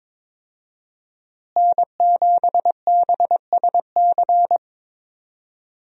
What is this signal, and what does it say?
Morse code, heard as a keyed tone, about 22 wpm, 700 Hz. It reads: N7BSC
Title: USA Call Signs - 5 Characters - Repeated 2x - 22wpm